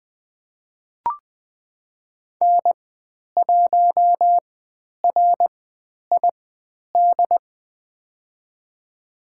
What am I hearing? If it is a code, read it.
Morse code, heard as a keyed tone, about 20 wpm, spaced as for 10 wpm, 700 Hz. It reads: N1RID
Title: 109 Lesson 18g - Letter m - Call Signs 20wpm